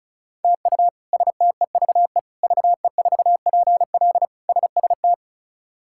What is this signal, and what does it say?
Morse code, heard as a keyed tone, about 36 wpm, 700 Hz. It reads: TU STEVE VE4PL SST
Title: K1USN SST QSO Practice - 35wpm